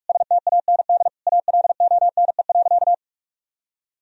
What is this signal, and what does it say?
Morse code, heard as a keyed tone, about 46 wpm, 700 Hz. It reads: STAND ALONE.
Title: Sentences from Top 1000 Words - Review Part 3 - 45wpm